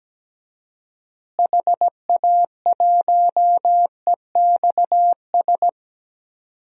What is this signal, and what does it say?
Morse code, heard as a keyed tone, about 17 wpm, 700 Hz. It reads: HA1EXS
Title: DXCC-100 Call Signs - 6 Characters - Repeated 3x - 17wpm